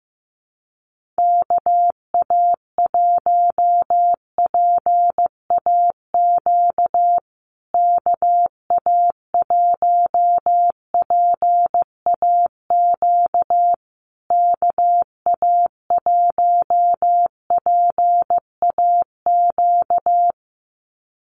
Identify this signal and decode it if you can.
Morse code, heard as a keyed tone, about 15 wpm, 700 Hz. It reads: KA1PAQ KA1PAQ KA1PAQ
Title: USA Call Signs Repeated 3x 15wpm